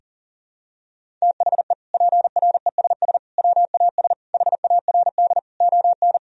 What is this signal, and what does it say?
Morse code, heard as a keyed tone, about 40 wpm, 700 Hz. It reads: THE PRESS WAS HARD ON
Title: Sentences from Top 1000 Words - Review Part 1 - 40wpm